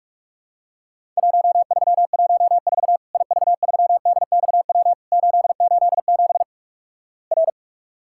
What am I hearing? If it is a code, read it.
Morse code, heard as a keyed tone, about 48 wpm, 700 Hz. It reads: 1314 IV3DXW 887 R
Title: Worked All Europe DX Contest - QTC Exchange - 45wpm